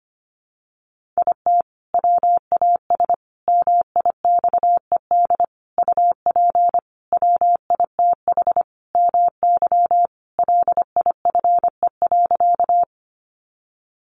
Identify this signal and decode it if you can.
Morse code, heard as a keyed tone, about 25 wpm, 700 Hz. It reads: IT WAS MIXED UP WITH MY LIFE.
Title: Sentences from Top 1500 Words 25wpm